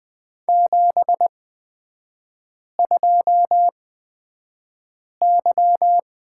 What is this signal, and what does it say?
Morse code, heard as a keyed tone, about 20 wpm, spaced as for 5 wpm, 700 Hz. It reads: 72Y
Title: 255 Lesson 41e - Period - Three Characters 20wpm